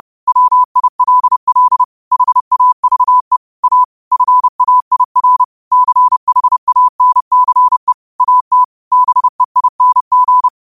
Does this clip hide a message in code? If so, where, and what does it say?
Yes: Morse code from 0.3 to 10.5 s, heard as a keyed tone, about 30 wpm, 1000 Hz. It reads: WILL HAVE A FAIR CHANCE AT BEING